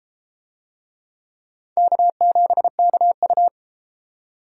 Morse code, heard as a keyed tone, about 32 wpm, 700 Hz. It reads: K7KU